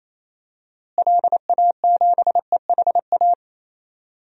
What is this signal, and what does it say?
Morse code, heard as a keyed tone, about 28 wpm, 700 Hz. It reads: LA7EHA
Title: DXCC-100 Call Signs - 6 Characters - Repeated 3x - 28wpm